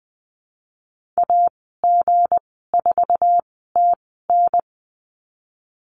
Morse code, heard as a keyed tone, about 19 wpm, spaced as for 14 wpm, 700 Hz. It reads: AG4TN
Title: 167 Lesson 27g - Number 7 - Call Signs 20wpm